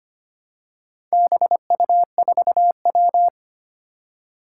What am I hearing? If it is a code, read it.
Morse code, heard as a keyed tone, about 25 wpm, 700 Hz. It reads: BU4W
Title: Four Challenging Letter-Number Combinations 25wpm